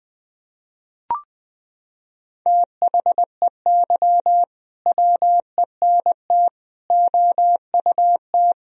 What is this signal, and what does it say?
Morse code, heard as a keyed tone, about 21 wpm, 700 Hz. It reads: THEY WENT OUT
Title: Sentences from Top 1000 Words - Review Part 3 - 20wpm